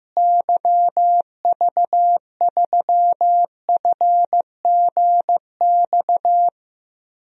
Morse code, heard as a keyed tone, about 15 wpm, 700 Hz. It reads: YV3FGX
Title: DXCC-100 Call Signs - 6 Characters - Repeated 2x - 15wpm